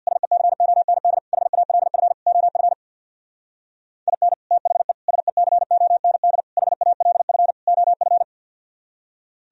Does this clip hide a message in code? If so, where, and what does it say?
Yes: Morse code from 0.1 to 8.2 s, heard as a keyed tone, about 65 wpm, 700 Hz. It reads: SECOND HALF OF IN THE SECOND HALF OF